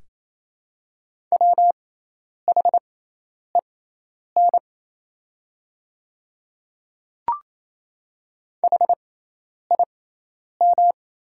Morse code, heard as a keyed tone, about 28 wpm, spaced as for 10 wpm, 700 Hz. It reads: WHEN HIM
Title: 6x Character Spacing - Top 100 Words - 28wpm